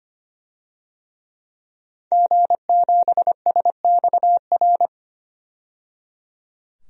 Morse code, heard as a keyed tone, about 25 wpm, 700 Hz. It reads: G7SXR
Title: DXCC-100 Call Signs - 5 Characters - 25wpm